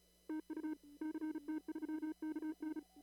Morse code, heard as a keyed tone, about 34 wpm, 310 Hz. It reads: TU CT3KN
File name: CW / kt5j wpx cw-op-k5pi